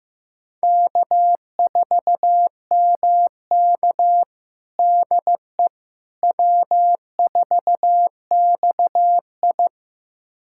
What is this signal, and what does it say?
Morse code, heard as a keyed tone, about 15 wpm, 700 Hz. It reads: K4MK DE W4XI
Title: USA Callsign DE Callsign - 4 Characters - 15wpm